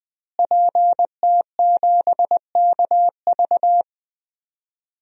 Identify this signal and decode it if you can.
Morse code, heard as a keyed tone, about 21 wpm, 700 Hz. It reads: PT7KV
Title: DXCC-100 Call Signs - 5 Characters - Repeated 3x - 20wpm